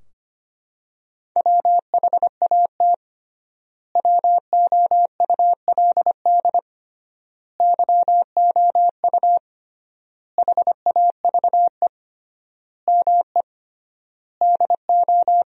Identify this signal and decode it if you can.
Morse code, heard as a keyed tone, about 25 wpm, 700 Hz. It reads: WHAT WOULD YOU HAVE ME DO
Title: 3x Word-Spacing - Sentences from Top 300 Words - 25wpm